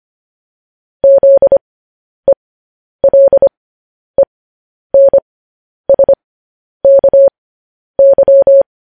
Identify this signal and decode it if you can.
Morse code, heard as a keyed tone, about 25 wpm, spaced as for 10 wpm, 550 Hz. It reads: ZELENSKY
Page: News Headlines in Morse Code at 10 WPM